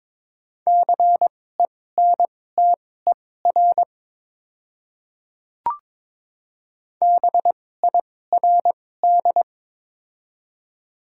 Morse code, heard as a keyed tone, about 22 wpm, spaced as for 16 wpm, 700 Hz. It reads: CENTER BIRD
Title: 2x Character Spacing - Top 500 Words - 22wpm